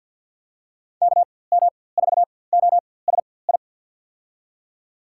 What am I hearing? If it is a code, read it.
Morse code, heard as a keyed tone, about 46 wpm, spaced as for 23 wpm, 700 Hz. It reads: KM4OSI
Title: USA Call Signs - 6 Characters - 4x Character Spacing - 50wpm